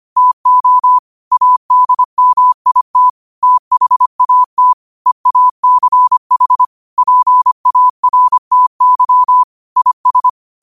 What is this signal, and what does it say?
Morse code, heard as a keyed tone, about 25 wpm, 1000 Hz. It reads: TO ADMIT THAT EACH PARTY IS